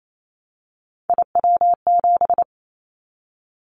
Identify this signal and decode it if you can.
Morse code, heard as a keyed tone, about 28 wpm, 700 Hz. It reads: IW7
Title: Callsign Prefixes for Top 25 DXCC - 28wpm